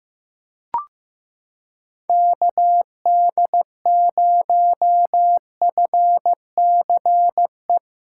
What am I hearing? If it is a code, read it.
Morse code, heard as a keyed tone, about 15 wpm, 700 Hz. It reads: KD0FCE
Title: USA Callsign DE Callsign 15wpm